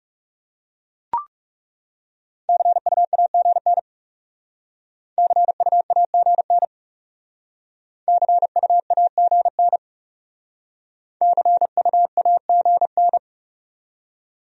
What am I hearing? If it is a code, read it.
Morse code, heard as a keyed tone, about 36 wpm, 700 Hz. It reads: CUAGN CUAGN CUAGN CUAGN
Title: Speed-Racing Abbreviations - 30wpm